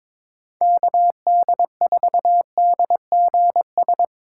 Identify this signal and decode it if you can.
Morse code, heard as a keyed tone, about 22 wpm, 700 Hz. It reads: KD4DGS
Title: USA Call Signs - 6 Characters - 2x Character Spacing - 22wpm